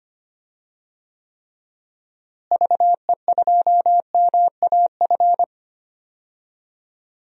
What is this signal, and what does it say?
Morse code, heard as a keyed tone, about 25 wpm, 700 Hz. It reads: VE2MAF
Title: DXCC-100 Call Signs - 6 Characters - Repeated 3x - 25wpm